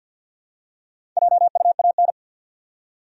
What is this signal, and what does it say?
Morse code, heard as a keyed tone, about 49 wpm, 700 Hz. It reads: JUAN
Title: USA Common Names - Max 5 Letters - 50wpm